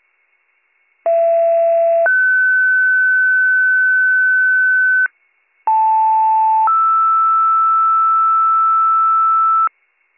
Two-tone page, heard as a one-second tone, then a three-second tone, about 670 then 1500 Hz, about 880 then 1320 Hz. It is heard from 1.1 to 9.7 s.